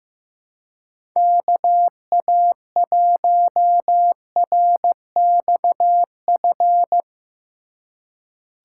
Morse code, heard as a keyed tone, about 15 wpm, 700 Hz. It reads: KA1RXF
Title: USA Call Signs - 6 Characters - 2x Character Spacing - 15wpm